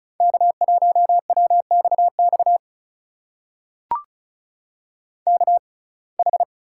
Morse code, heard as a keyed tone, about 36 wpm, spaced as for 25 wpm, 700 Hz. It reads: K1WXX KH
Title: USA Call Signs - 5 Characters - 6x Character Spacing - 35wpm